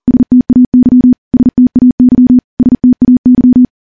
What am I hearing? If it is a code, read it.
Morse code, heard as a keyed tone, about 40 wpm, 250 Hz. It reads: STAY STAY STAY
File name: morse-training-40WPM